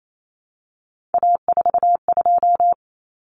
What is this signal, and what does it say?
Morse code, heard as a keyed tone, about 28 wpm, 700 Hz. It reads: A42
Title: Common Cut Numbers - 3-Digit 28wpm